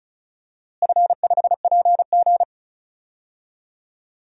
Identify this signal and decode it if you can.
Morse code, heard as a keyed tone, about 36 wpm, 700 Hz. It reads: F5PG
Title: DXCC-100 Call Signs - 4 Characters - Repeated 2x - 35wpm